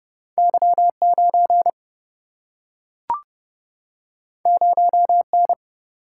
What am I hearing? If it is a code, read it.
Morse code, heard as a keyed tone, about 30 wpm, 700 Hz. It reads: Y9 0N
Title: Single-Letter Single-Number 30wpm